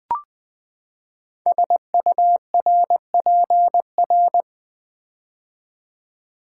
Morse code, heard as a keyed tone, about 20 wpm, 700 Hz. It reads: SURPR
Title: Build A Word - 5 Letters At A Time - 10-Letter Words - 20wpm